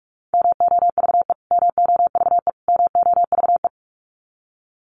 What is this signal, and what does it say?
Morse code, heard as a keyed tone, about 44 wpm, 700 Hz. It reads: MOVE MOVE MOVE
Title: Top 200 Words Repeated 45wpm